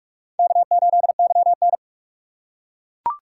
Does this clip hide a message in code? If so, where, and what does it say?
Yes: Morse code from 0.4 to 1.8 s, heard as a keyed tone, about 38 wpm, 700 Hz. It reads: K8YN